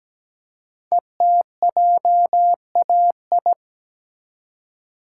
Morse code, heard as a keyed tone, about 17 wpm, 700 Hz. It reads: ETJAI
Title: Code Groups - 5 Letters - 17wpm